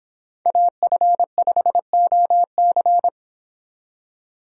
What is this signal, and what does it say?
Morse code, heard as a keyed tone, about 28 wpm, 700 Hz. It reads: AF5OC